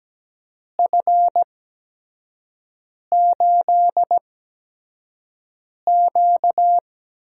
Morse code, heard as a keyed tone, about 17 wpm, spaced as for 5 wpm, 700 Hz. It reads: F8Q
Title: USA Call Signs - 4 Characters - 8x Character Spacing - 17wpm